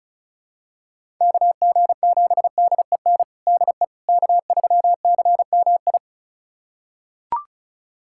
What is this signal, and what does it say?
Morse code, heard as a keyed tone, about 34 wpm, 700 Hz. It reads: KG7DEN DE K3CMI
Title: USA Callsign DE Callsign 35wpm